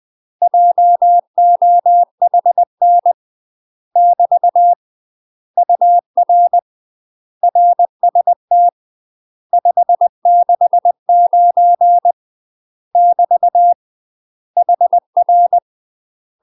Morse code, heard as a keyed tone, about 20 wpm, 700 Hz. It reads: JOHN <BT> UR RST 569 <BT> HR
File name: UK MASTER QSOs TRAINING/20WPM